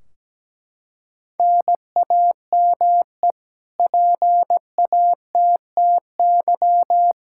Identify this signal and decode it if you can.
Morse code, heard as a keyed tone, about 17 wpm, 700 Hz. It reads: NAME PATTY
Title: Practice QSO - Exchange 1 - Beginning - 17wpm